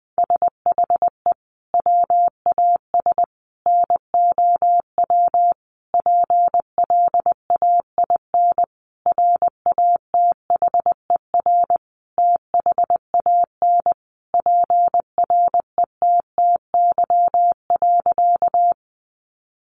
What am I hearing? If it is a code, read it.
Morse code, heard as a keyed tone, about 20 wpm, 700 Hz. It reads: SHE WAS NOW PLAIN RATHER THAN PRETTY.
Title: Sentences from Top 1000 Words 20wpm